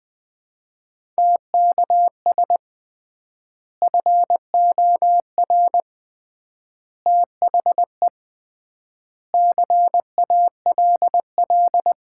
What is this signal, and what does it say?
Morse code, heard as a keyed tone, about 21 wpm, 700 Hz. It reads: TKS FOR THE CALL
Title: Common QSO Phrases - 3x Word-Spacing - 20wpm